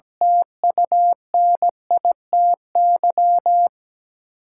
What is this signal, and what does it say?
Morse code, heard as a keyed tone, about 17 wpm, 700 Hz. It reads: TUNITY
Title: Sentences from Top 1000 Words - New Word Drill - 17wpm